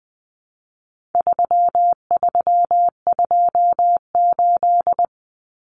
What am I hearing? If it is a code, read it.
Morse code, heard as a keyed tone, about 20 wpm, 700 Hz. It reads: 3328
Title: Common Cut Numbers - 4-Digit 20wpm